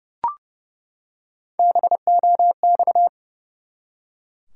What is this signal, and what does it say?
Morse code, heard as a keyed tone, about 30 wpm, 700 Hz. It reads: BOX
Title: Three Letter Words - 30wpm